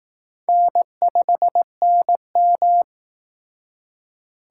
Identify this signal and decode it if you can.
Morse code, heard as a keyed tone, about 18 wpm, 700 Hz. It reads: N5NM
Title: Speed Racing - USA 1x2 Call Signs 15wpm